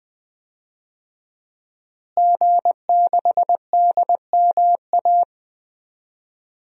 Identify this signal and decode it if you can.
Morse code, heard as a keyed tone, about 20 wpm, 700 Hz. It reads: G6DMA
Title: DXCC-100 Call Signs - 5 Characters - Repeated 3x - 20wpm